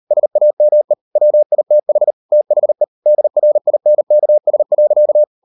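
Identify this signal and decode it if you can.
Morse code, heard as a keyed tone, about 38 wpm, 600 Hz. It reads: SAME WITH THE DRINKS.
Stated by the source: Morse code practice at 38 WPM, sending Ken Follett's novel Never in English, from CW med Gnister.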